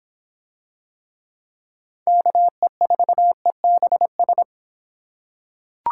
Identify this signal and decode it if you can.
Morse code, heard as a keyed tone, about 24 wpm, 700 Hz. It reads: KE4EBS